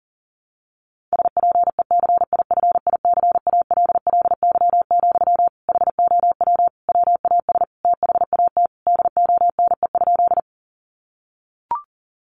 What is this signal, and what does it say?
Morse code, heard as a keyed tone, about 40 wpm, 700 Hz. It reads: SPECIFICALLY, HOW WAS THAT DONE?
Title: Sentences from Top 1100 Words 40wpm